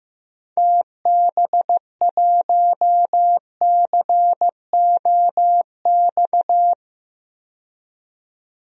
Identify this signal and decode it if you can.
Morse code, heard as a keyed tone, about 16 wpm, 700 Hz. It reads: TB1COX
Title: DXCC-100 Call Signs - 6 Characters - 15wpm